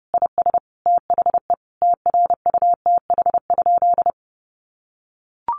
Morse code, heard as a keyed tone, about 30 wpm, 700 Hz. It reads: IS THE TRUTH?
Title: Sentences from Top 1000 Words 30wpm